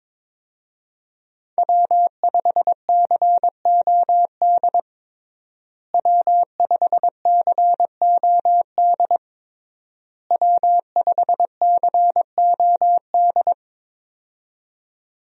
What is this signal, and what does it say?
Morse code, heard as a keyed tone, about 22 wpm, 700 Hz. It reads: W5COD W5COD W5COD
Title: USA Call Signs - 5 Characters - Repeated 3x - 22wpm